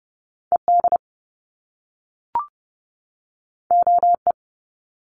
Morse code, heard as a keyed tone, about 30 wpm, 700 Hz. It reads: ED OE
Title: Top Two Letter Combinations in Words 30wpm